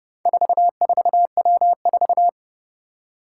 Morse code, heard as a keyed tone, about 30 wpm, 700 Hz. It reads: 44W4
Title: Four Challenging Letter-Number Combinations 30wpm